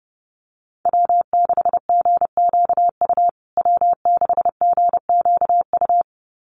Morse code, heard as a keyed tone, about 30 wpm, 700 Hz. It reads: W6GQU W6GQU
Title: USA Call Signs Repeated 2x 30wpm